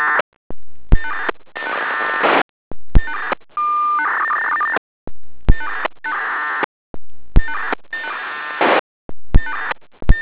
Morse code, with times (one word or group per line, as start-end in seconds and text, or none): none